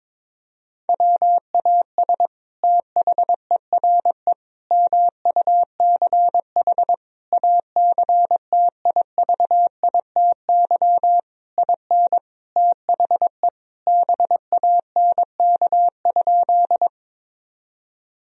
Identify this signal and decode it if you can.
Morse code, heard as a keyed tone, about 22 wpm, 700 Hz. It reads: WAS THERE MUCH ACTIVITY IN THE BANK?
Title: Sentences from Top 1000 Words - Review Part 4 - 22wpm